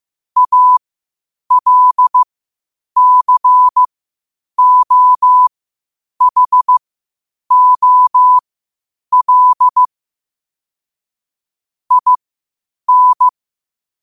Morse code, heard as a keyed tone, about 15 wpm, spaced as for 8 wpm, 1000 Hz. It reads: ALCOHOL IN
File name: qotdfiles / 8wpm